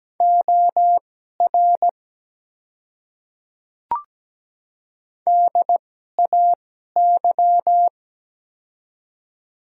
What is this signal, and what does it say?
Morse code, heard as a keyed tone, about 17 wpm, spaced as for 12 wpm, 700 Hz. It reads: OR DAY
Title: 2x Character Spacing - Top 100 Words - 17wpm